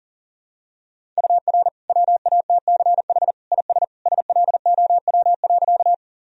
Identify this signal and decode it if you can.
Morse code, heard as a keyed tone, about 40 wpm, 700 Hz. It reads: UR WATCH IS SLOW.
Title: Sentences from Top 400 Words and Abbreviations 40wpm